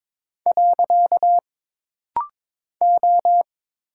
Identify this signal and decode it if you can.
Morse code, heard as a keyed tone, about 21 wpm, 700 Hz. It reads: . O
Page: This rapid-fire timing is geared to push your ICR (Instant Character Recognition) skills.